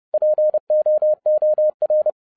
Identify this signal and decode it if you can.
Morse code, heard as a keyed tone, about 30 wpm, 610 Hz. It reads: POOR